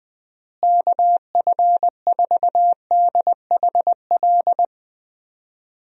Morse code, heard as a keyed tone, about 20 wpm, 700 Hz. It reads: KF4DHL
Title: USA Call Signs - 6 Characters - Repeated 2x - 20wpm